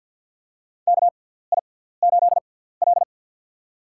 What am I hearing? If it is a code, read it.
Morse code, heard as a keyed tone, about 51 wpm, 700 Hz. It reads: K I 8 L
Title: USA Call Signs - 6x Character Spacing - 50wpm